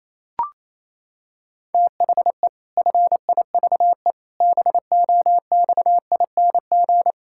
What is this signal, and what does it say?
Morse code, heard as a keyed tone, about 28 wpm, 700 Hz. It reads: THE FIVE BOXING
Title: Pangram Challenge v2 - 28wpm